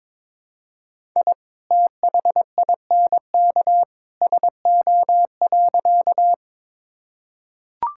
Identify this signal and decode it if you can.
Morse code, heard as a keyed tone, about 22 wpm, 700 Hz. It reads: I THINK SO.
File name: practice / 22wpm / Sentences with 2-Word Phrases - 22wpm